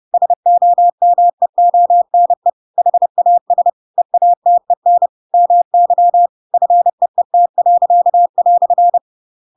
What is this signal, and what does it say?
Morse code, heard as a keyed tone, about 30 wpm, 700 Hz. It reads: SOMEONE HAS EATEN MY FEET."
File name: Pinocchio / Pinocchio ch7 30 30 ik1hge